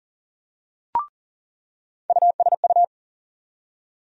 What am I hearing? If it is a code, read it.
Morse code, heard as a keyed tone, about 40 wpm, 700 Hz. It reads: USU
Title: USA Call Sign Postfixes - 3 Letters 40wpm